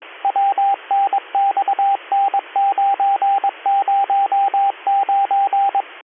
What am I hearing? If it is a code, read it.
Morse code, heard as a keyed tone, about 22 wpm, 790 Hz, with radio noise. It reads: WNXN909
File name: 202- .0-call 11776